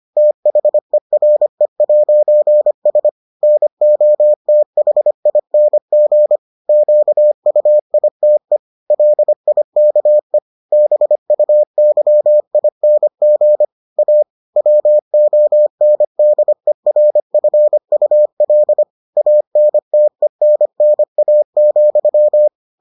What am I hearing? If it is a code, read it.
Morse code, heard as a keyed tone, about 25 wpm, 600 Hz. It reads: THERE'S NOTHING QUITE LIKE BUYING A WONDERFUL ANTENNA,